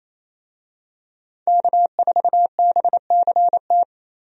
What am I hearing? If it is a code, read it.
Morse code, heard as a keyed tone, about 28 wpm, 700 Hz. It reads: K4BCT
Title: USA Call Signs - 5 Characters - Repeated 3x - 28wpm